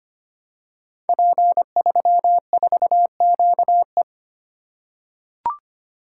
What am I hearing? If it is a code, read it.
Morse code, heard as a keyed tone, about 25 wpm, 700 Hz. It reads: P34QE
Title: Code Groups - 5 Letter:Numbers - 25wpm